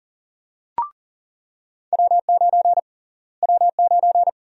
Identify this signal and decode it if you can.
Morse code, heard as a keyed tone, about 40 wpm, 700 Hz. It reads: W9 W9
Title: USA Call Sign Prefixes - Repeated 2x - 40wpm